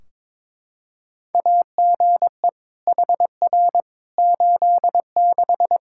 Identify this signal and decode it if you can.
Morse code, heard as a keyed tone, about 21 wpm, 700 Hz. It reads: AGE HR 86
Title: Practice QSO - Exchange 3 - More Info - 22wpm